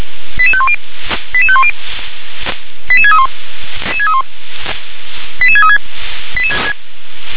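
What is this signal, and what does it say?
Five-tone selective call (ZVEI), heard as quick runs of five tones, tones of about 70 ms. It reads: 88510 88510 88631 88636
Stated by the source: ZVEI five-tone selective call received by SDR.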